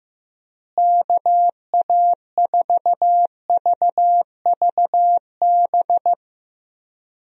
Morse code, heard as a keyed tone, about 15 wpm, 700 Hz. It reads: KA4VVB
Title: USA Call Signs - 6 Characters - Repeated 2x - 15wpm